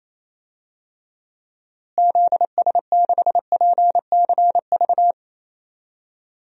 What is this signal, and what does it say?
Morse code, heard as a keyed tone, about 28 wpm, 700 Hz. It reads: ZS6PCV
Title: DXCC-100 Call Signs - 6 Characters - Repeated 2x - 28wpm